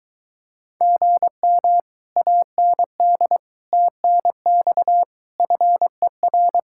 Morse code, heard as a keyed tone, about 23 wpm, 700 Hz. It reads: GM AND TNX FER